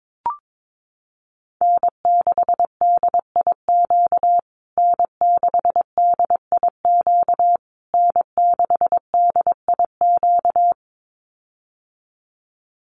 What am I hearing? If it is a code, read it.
Morse code, heard as a keyed tone, about 22 wpm, 700 Hz. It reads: N6DIQ N6DIQ N6DIQ
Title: USA Call Signs Repeated 3x 22wpm